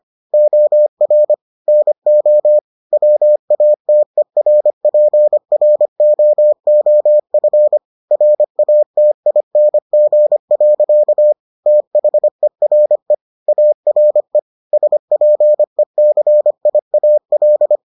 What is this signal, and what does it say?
Morse code, heard as a keyed tone, about 25 wpm, 600 Hz. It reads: OR NO WATERPROOF RATING. THERE ARE SPECIAL